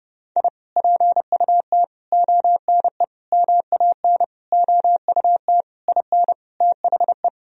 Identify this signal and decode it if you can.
Morse code, heard as a keyed tone, about 30 wpm, 700 Hz. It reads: I PUT ONE MAN OUT IN THE
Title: Sentences from Top 1000 Words - Review Part 2 - 30wpm